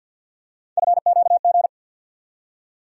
Morse code, heard as a keyed tone, about 52 wpm, 700 Hz. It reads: FQG
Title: USA Call Sign Postfixes - 3 Letters 50wpm